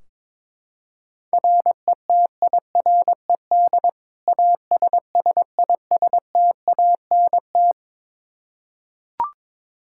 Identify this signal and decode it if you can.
Morse code, heard as a keyed tone, about 22 wpm, 700 Hz. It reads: RETIRED ASSISTANT
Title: Practice QSO - Exchange 3 - More Info - 22wpm